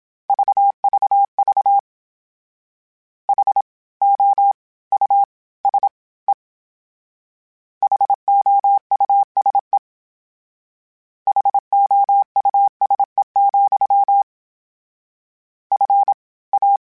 Morse code, heard as a keyed tone, about 27 wpm, 800 Hz. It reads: VVV H O U S E HOUSE HOUSE, F A
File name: General / General-mp3s / Houses @25